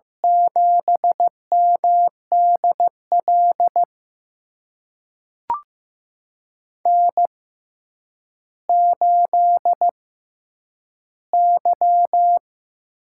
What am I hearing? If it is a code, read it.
Morse code, heard as a keyed tone, about 15 wpm, spaced as for 9 wpm, 700 Hz. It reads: 7MDL N8Y
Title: USA Call Signs - 5 Characters - 6x Character Spacing - 15wpm